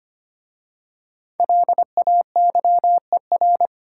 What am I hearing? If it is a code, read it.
Morse code, heard as a keyed tone, about 25 wpm, 700 Hz. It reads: LAYER
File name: 5 Letter Words - First 3 Letters Given - 25wpm